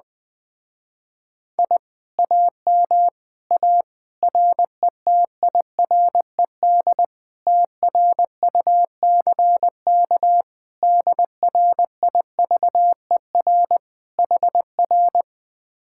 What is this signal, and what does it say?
Morse code, heard as a keyed tone, about 20 wpm, 700 Hz. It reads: I AM A RETIRED TRUCK DRIVER HR
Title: Speed-Racing Common QSO Phrases - 17wpm